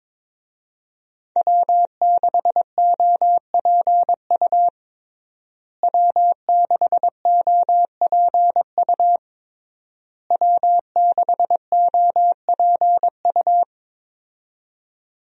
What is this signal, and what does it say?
Morse code, heard as a keyed tone, about 22 wpm, 700 Hz. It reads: W6OPU W6OPU W6OPU
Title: USA Call Signs - 5 Characters - Repeated 3x - 22wpm